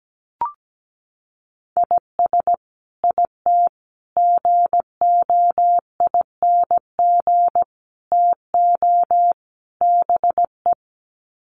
Morse code, heard as a keyed tone, about 17 wpm, 700 Hz. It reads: IS IT GOING TO BE
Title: Sets of 5 Words Encore2 17wpm